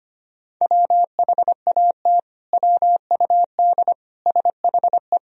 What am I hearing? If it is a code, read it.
Morse code, heard as a keyed tone, about 25 wpm, 700 Hz. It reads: WHAT WUD SHE
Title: Sentences from Top 100 Words and Abbreviations 25wpm